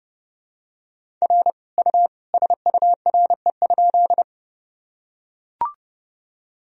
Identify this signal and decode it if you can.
Morse code, heard as a keyed tone, about 31 wpm, 700 Hz. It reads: R U SURE?
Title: Sentences from Top 300 Words and Abbreviations 30wpm